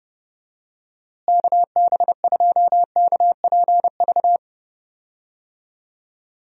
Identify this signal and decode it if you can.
Morse code, heard as a keyed tone, about 30 wpm, 700 Hz. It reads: KB2KPV
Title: USA Call Signs - 6 Characters - Repeated 3x - 30wpm